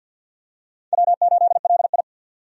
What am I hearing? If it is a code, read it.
Morse code, heard as a keyed tone, about 46 wpm, 700 Hz. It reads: W8LI
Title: USA Call Signs - 4 Characters - 50wpm